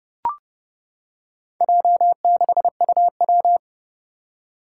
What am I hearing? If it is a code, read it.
Morse code, heard as a keyed tone, about 30 wpm, 700 Hz. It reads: J6UW
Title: Four Challenging Letter-Number Combinations 30wpm